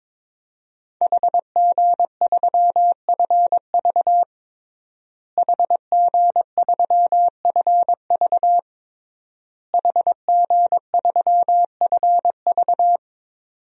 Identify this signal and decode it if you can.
Morse code, heard as a keyed tone, about 22 wpm, 700 Hz. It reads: HG3FV HG3FV HG3FV
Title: DXCC-100 Call Signs - 5 Characters - Repeated 3x - 22wpm